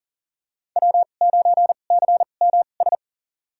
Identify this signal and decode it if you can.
Morse code, heard as a keyed tone, about 40 wpm, spaced as for 29 wpm, 700 Hz. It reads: W9CMS
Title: USA Call Signs - 5 Characters - 2x Character Spacing - 40wpm